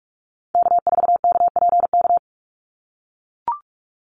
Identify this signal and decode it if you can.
Morse code, heard as a keyed tone, about 44 wpm, 700 Hz. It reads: K4KPK
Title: NA SOTA Activator Callsigns 45wpm